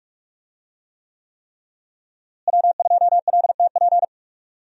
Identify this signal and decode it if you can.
Morse code, heard as a keyed tone, about 46 wpm, 700 Hz. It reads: W2LTP
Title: USA Call Signs - 5 Characters - Repeated 3x - 45wpm